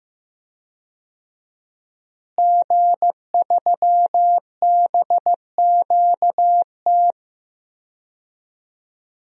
Morse code, heard as a keyed tone, about 15 wpm, 700 Hz. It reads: G3BQT
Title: DXCC-100 Call Signs - 5 Characters - Repeated 3x - 15wpm